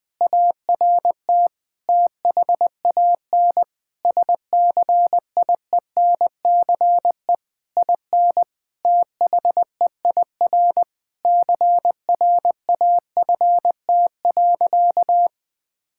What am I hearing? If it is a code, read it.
Morse code, heard as a keyed tone, about 21 wpm, 700 Hz. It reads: ART THAN SCIENCE IN THEIR CRAFT.
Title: Sentences from Top 1000 Words - Review Part 6 - 20wpm